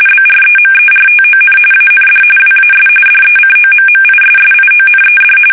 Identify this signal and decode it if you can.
RTTY, about 45.45 baud. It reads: BREMERTON AND WAS DECOMMISSIONED
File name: fsk850 / battleshipnj